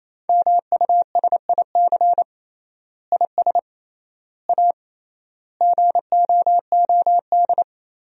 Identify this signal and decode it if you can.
Morse code, heard as a keyed tone, about 28 wpm, 700 Hz. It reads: MUSIC IS A GOOD
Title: 3x Word-Spacing - Sentences from Top 300 Words - 28wpm